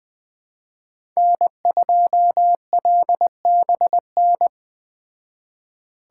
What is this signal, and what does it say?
Morse code, heard as a keyed tone, about 19 wpm, 700 Hz. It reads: N2LBN